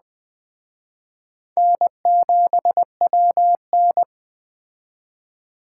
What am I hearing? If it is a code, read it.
Morse code, heard as a keyed tone, about 19 wpm, 700 Hz. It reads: N7WN